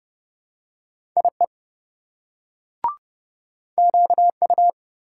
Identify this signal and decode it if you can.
Morse code, heard as a keyed tone, about 26 wpm, 700 Hz. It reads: IE QU